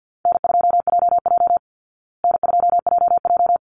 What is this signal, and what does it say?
Morse code, heard as a keyed tone, about 50 wpm, 700 Hz. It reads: N2JJ N2JJ
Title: Easy Call Signs Repeated 2x - 50wpm